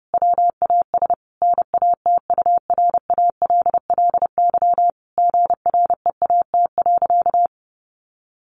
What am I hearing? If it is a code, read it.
Morse code, heard as a keyed tone, about 30 wpm, 700 Hz. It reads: WAS NATURALLY GREAT.
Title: Sentences from Top 1300 Words 30wpm